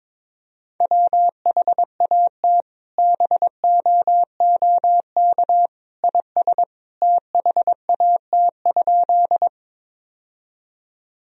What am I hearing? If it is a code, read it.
Morse code, heard as a keyed tone, about 22 wpm, 700 Hz. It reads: WHAT BOOK IS THAT?
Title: Sentences from Top 1000 Words - Review Part 1 - 22wpm